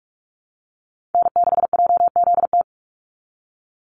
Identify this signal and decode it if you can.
Morse code, heard as a keyed tone, about 44 wpm, 700 Hz. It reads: N6JZT
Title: NA SOTA Chaser Callsigns 45wpm